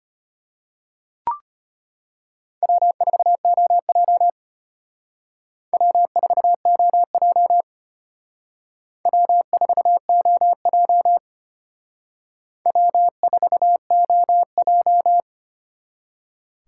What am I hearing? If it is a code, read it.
Morse code, heard as a keyed tone, about 32 wpm, 700 Hz. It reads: W4OJ W4OJ W4OJ W4OJ